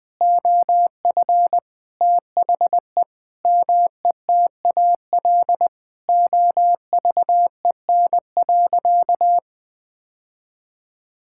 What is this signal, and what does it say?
Morse code, heard as a keyed tone, about 20 wpm, 700 Hz. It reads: OF THE METAL OVEN.
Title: Sentences from Top 1000 Words - Review Part 6 - 20wpm